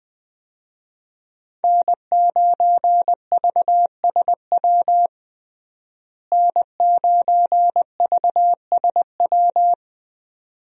Morse code, heard as a keyed tone, about 20 wpm, 700 Hz. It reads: N9VSW N9VSW
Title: USA Call Signs - 5 Characters - Repeated 2x - 20wpm